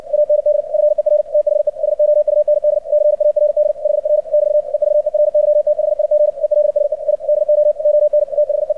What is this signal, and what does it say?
Morse code, heard as a keyed tone, about 56 wpm, 600 Hz, with radio noise. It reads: AND FER THE SOUND CARD HR <BT> EVERYTHIG ELSE I DO ON THE
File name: archived / CRSnet